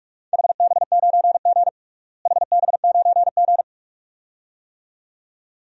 Morse code, heard as a keyed tone, about 46 wpm, 700 Hz. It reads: HB9G HB9G
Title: DXCC-100 Call Signs - 4 Characters - Repeated 2x - 45wpm